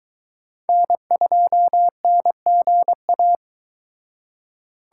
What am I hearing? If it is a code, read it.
Morse code, heard as a keyed tone, about 22 wpm, 700 Hz. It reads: N2NGA